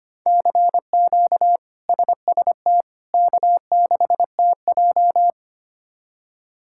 Morse code, heard as a keyed tone, about 25 wpm, 700 Hz. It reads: CQ SST K6TJ
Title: K1USN SST QSO Practice - 25wpm